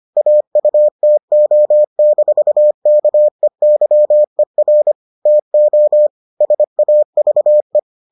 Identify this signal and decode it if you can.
Morse code, heard as a keyed tone, about 25 wpm, 600 Hz. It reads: AUTO-KEYER TO SAVE